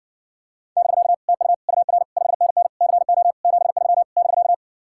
Morse code, heard as a keyed tone, about 69 wpm, 700 Hz. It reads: <BK> TU UR 5NN CO 73 <BK>